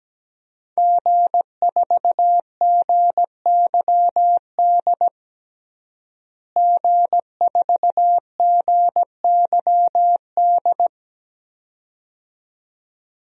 Morse code, heard as a keyed tone, about 17 wpm, 700 Hz. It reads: G4GYD G4GYD
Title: DXCC-100 Call Signs - 5 Characters - Repeated 2x - 17wpm